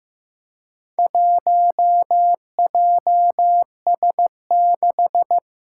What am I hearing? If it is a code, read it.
Morse code, heard as a keyed tone, about 15 wpm, 700 Hz. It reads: 1JS6
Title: Four Challenging Letter-Number Combinations 15wpm